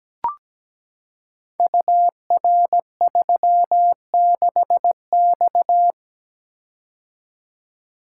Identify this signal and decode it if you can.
Morse code, heard as a keyed tone, about 17 wpm, 700 Hz. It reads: UR36X
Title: Code Groups - 5 Letter:Numbers - 17wpm